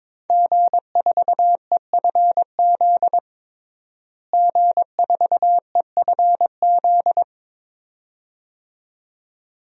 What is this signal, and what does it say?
Morse code, heard as a keyed tone, about 22 wpm, 700 Hz. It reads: G4EFZ G4EFZ
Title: DXCC-100 Call Signs - 5 Characters - Repeated 2x - 22wpm